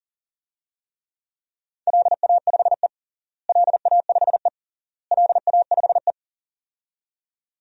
Morse code, heard as a keyed tone, about 40 wpm, 700 Hz. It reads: LA5E LA5E LA5E
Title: DXCC-100 Call Signs - 4 Characters - Repeated 3x - 40wpm